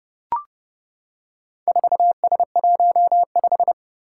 Morse code, heard as a keyed tone, about 30 wpm, 700 Hz. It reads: 4S15
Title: Four Challenging Letter-Number Combinations 30wpm